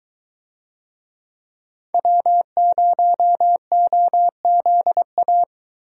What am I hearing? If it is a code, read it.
Morse code, heard as a keyed tone, about 23 wpm, 700 Hz. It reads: W0OZA